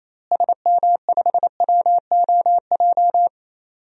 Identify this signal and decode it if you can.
Morse code, heard as a keyed tone, about 27 wpm, 700 Hz. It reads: SM5WOJ
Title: DXCC-100 Call Signs - 6 Characters - Repeated 3x - 28wpm